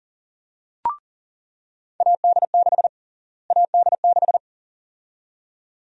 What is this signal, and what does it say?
Morse code, heard as a keyed tone, about 40 wpm, 700 Hz. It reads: AD6 AD6
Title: USA Call Sign Prefixes - Repeated 2x - 40wpm